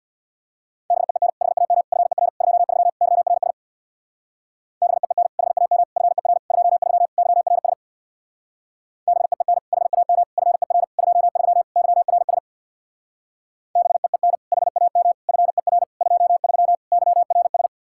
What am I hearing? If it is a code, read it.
Morse code, heard as a keyed tone, about 59 wpm, 700 Hz. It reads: BEEN HAM FER 23 YRS BEEN HAM FER 23 YRS BEEN HAM FER 23 YRS BEEN HAM FER 23 YRS
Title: Speed-Racing Common QSO Phrases - 50wpm